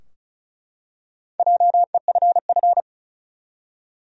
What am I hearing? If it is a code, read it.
Morse code, heard as a keyed tone, about 34 wpm, 700 Hz. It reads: JEFF
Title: USA Common Names - 35wpm